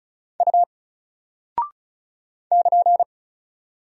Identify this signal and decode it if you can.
Morse code, heard as a keyed tone, about 36 wpm, 700 Hz. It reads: U <KN>
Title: Single Character (Letter:Number:Punctuation:Prosign) Alt 35wpm